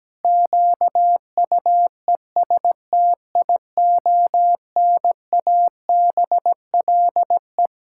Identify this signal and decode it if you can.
Morse code, heard as a keyed tone, about 17 wpm, 700 Hz. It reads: QUESTIONABLE
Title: Pangram Challenge v2 - 17wpm